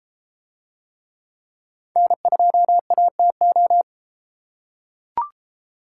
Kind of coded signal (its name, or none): Morse code